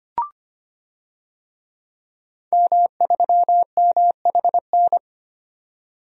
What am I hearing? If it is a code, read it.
Morse code, heard as a keyed tone, about 25 wpm, 700 Hz. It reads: M3MHN
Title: DXCC-100 Call Signs - 5 Characters - 25wpm